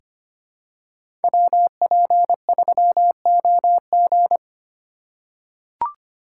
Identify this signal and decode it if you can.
Morse code, heard as a keyed tone, about 25 wpm, 700 Hz. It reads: WP3OG
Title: USA Call Signs - 5 Characters - 3x Character Spacing - 25wpm